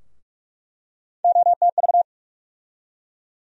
Morse code, heard as a keyed tone, about 43 wpm, 700 Hz. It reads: OTV
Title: USA Call Sign Postfixes - 3 Letters 45wpm